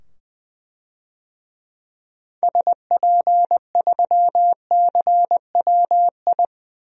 Morse code, heard as a keyed tone, about 20 wpm, 700 Hz. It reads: SP3CWI
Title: DXCC-100 Call Signs - 6 Characters - 20wpm